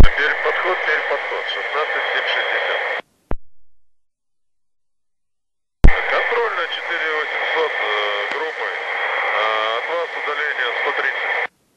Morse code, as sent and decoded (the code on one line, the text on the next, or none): none